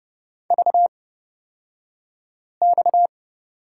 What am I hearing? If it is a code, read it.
Morse code, heard as a keyed tone, about 30 wpm, spaced as for 5 wpm, 700 Hz. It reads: VX
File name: Lesson 37d - Letter x - Two Characters 30wpm